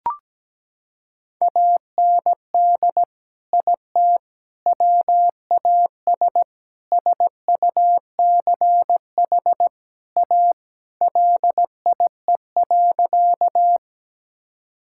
Morse code, heard as a keyed tone, about 17 wpm, 700 Hz. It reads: AND IT WAS SUCH A LIE.
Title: Sentences from Top 1600 Words 17wpm